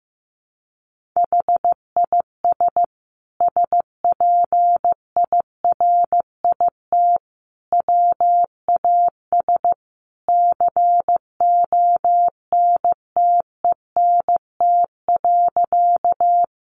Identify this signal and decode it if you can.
Morse code, heard as a keyed tone, about 14 wpm, 700 Hz. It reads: HIS SPIRIT WAS CONTENT.
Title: Sentences from Top 1400 Words 15wpm